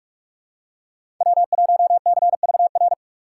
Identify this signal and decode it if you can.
Morse code, heard as a keyed tone, about 46 wpm, 700 Hz. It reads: W1CVR
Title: USA Call Signs - 6x Character Spacing - 45wpm